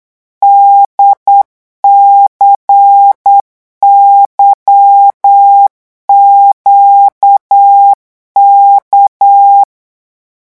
Morse code, heard as a keyed tone, about 8 wpm, 800 Hz. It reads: DCYQK
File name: morse mp3 / revision3-vitesse 8 mots